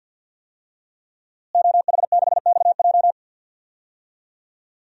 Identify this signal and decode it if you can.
Morse code, heard as a keyed tone, about 48 wpm, 700 Hz. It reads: OH6XJ
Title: DXCC-100 Call Signs - 5 Characters - Repeated 2x - 50wpm